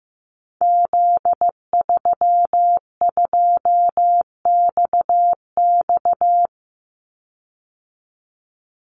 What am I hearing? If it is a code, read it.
Morse code, heard as a keyed tone, about 15 wpm, 700 Hz. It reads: Z32XX
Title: Call Signs Repeated 2x - 15wpm